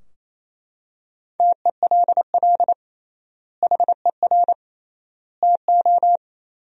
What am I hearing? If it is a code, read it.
Morse code, heard as a keyed tone, about 28 wpm, 700 Hz. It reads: TELL HER TO
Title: 3x Word-Spacing - Sentences from Top 300 Words - 28wpm